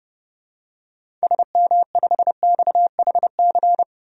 Morse code, heard as a keyed tone, about 30 wpm, 700 Hz. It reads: SM5XHC
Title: DXCC-100 Call Signs - 6 Characters - 30wpm